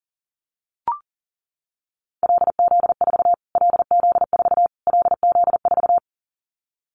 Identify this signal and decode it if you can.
Morse code, heard as a keyed tone, about 40 wpm, 700 Hz. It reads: LZ4 LZ4 LZ4
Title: Callsign Prefixes for Top 25 DXCC - Repeated 3x - 40wpm